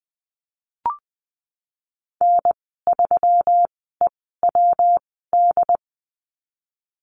Morse code, heard as a keyed tone, about 20 wpm, spaced as for 15 wpm, 700 Hz. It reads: N3EWD
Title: 122 Lesson 20g - Number 3 - Call Signs 20wpm